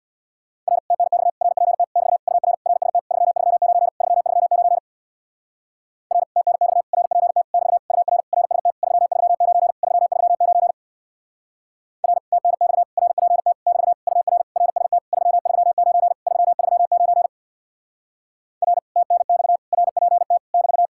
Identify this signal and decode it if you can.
Morse code, heard as a keyed tone, about 64 wpm, 700 Hz. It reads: R TNX RPT <BT> UR RST 339 339 R TNX RPT <BT> UR RST 339 339 R TNX RPT <BT> UR RST 339 339 R TNX RPT <BT>